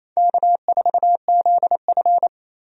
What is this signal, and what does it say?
Morse code, heard as a keyed tone, about 28 wpm, 700 Hz. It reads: K4ZF
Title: USA Call Signs - 4 Characters - 28wpm